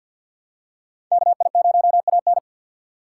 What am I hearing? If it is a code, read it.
Morse code, heard as a keyed tone, about 52 wpm, 700 Hz. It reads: KI0AN
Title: USA Call Signs - 5 Characters - 3x Character Spacing - 50wpm